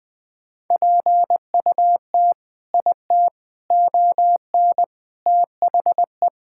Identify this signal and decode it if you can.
Morse code, heard as a keyed tone, about 20 wpm, 700 Hz. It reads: PUT IT ON THE
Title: Sentences from Top 1000 Words - Review Part 5 - 20wpm